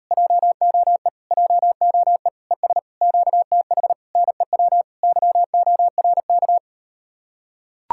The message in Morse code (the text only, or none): JOE JOE ES QTH NEW YORK